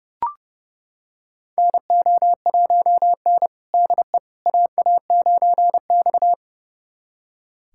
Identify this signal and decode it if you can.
Morse code, heard as a keyed tone, about 30 wpm, 700 Hz. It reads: NO1N DE AA9X
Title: USA Callsign DE Callsign - 4 Characters - 30wpm